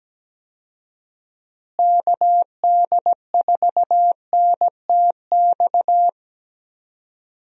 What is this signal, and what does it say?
Morse code, heard as a keyed tone, about 17 wpm, 700 Hz. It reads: KD4NTX
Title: USA Call Signs - 6 Characters - Repeated 2x - 17wpm